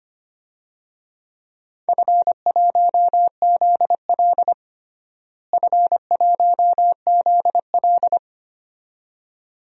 Morse code, heard as a keyed tone, about 25 wpm, 700 Hz. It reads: F1ZL F1ZL
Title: DXCC-100 Call Signs - 4 Characters - Repeated 2x - 25wpm